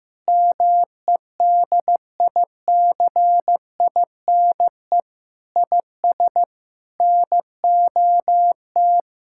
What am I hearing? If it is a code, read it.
Morse code, heard as a keyed tone, about 15 wpm, 700 Hz. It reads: MEDICINE IS NOT
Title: Sentences from Top 1000 Words - Review Part 6 - 15wpm